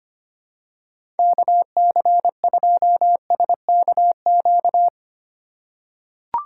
Morse code, heard as a keyed tone, about 25 wpm, 700 Hz. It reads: KC2SKQ
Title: USA Call Signs - 6 Characters - 3x Character Spacing - 25wpm